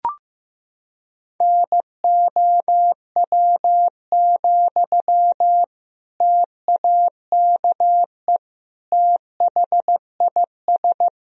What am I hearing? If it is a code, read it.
Morse code, heard as a keyed tone, about 16 wpm, 700 Hz. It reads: NOW, TAKE THIS
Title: Sentences from Top 400 Words and Abbreviations 15wpm